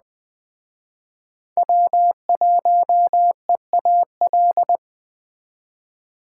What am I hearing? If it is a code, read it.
Morse code, heard as a keyed tone, about 20 wpm, 700 Hz. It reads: W1EAL
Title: Speed Racing - USA Call Signs 17wpm